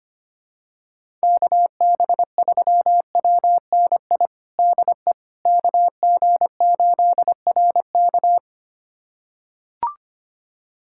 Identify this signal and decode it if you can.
Morse code, heard as a keyed tone, about 25 wpm, 700 Hz. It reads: KB3WNI DE KG8RK
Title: USA Callsign DE Callsign 25wpm